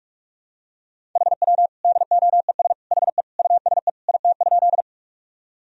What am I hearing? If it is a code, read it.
Morse code, heard as a keyed tone, about 46 wpm, 700 Hz. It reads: HW DOES HE USE IT?
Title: Sentences from Top 200 Words and Abbreviations 45wpm